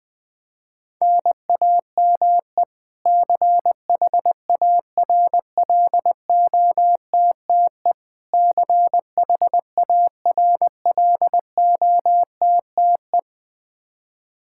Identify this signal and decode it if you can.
Morse code, heard as a keyed tone, about 20 wpm, 700 Hz. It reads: NAME CHARLOTTE CHARLOTTE
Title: Practice QSO - Exchange 1 - Beginning - 20wpm